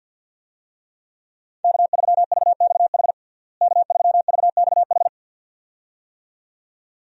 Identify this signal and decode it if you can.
Morse code, heard as a keyed tone, about 53 wpm, 700 Hz. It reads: K3VXH K3VXH
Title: USA Call Signs - 5 Characters - Repeated 2x - 50wpm